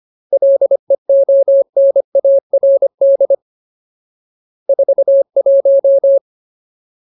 Morse code, heard as a keyed tone, about 25 wpm, 550 Hz. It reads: LEONARD 41
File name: Ultimate Word List / 25WPM Ultimate